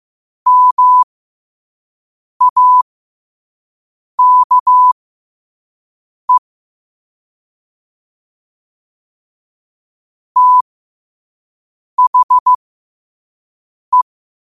Morse code, heard as a keyed tone, about 15 wpm, spaced as for 5 wpm, 1000 Hz. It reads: MAKE THE
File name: qotdfiles / 5wpm